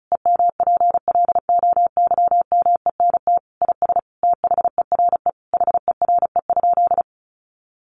Morse code, heard as a keyed tone, about 34 wpm, 700 Hz. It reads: EMPLOYMENT IS THERE HERE?
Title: Sentences from Top 1500 Words 35wpm